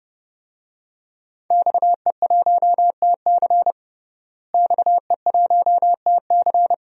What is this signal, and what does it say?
Morse code, heard as a keyed tone, about 29 wpm, 700 Hz. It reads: XE1TC XE1TC
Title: DXCC-100 Call Signs - 5 Characters - Repeated 3x - 30wpm